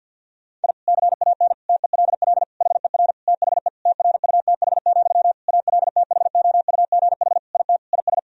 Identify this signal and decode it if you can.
Morse code, heard as a keyed tone, about 49 wpm, 700 Hz. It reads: I CAN TELL HER THE TRUTH, ALTHOUGH IT IS
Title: Sentences from Top 1000 Words - Review Part 3 - 50wpm